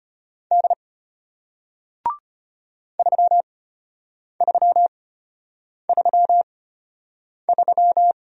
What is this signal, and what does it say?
Morse code, heard as a keyed tone, about 33 wpm, spaced as for 8 wpm, 700 Hz. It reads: D 3333